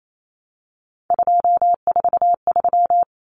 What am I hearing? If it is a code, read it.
Morse code, heard as a keyed tone, about 28 wpm, 700 Hz. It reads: 243
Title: Common Cut Numbers - 3-Digit 28wpm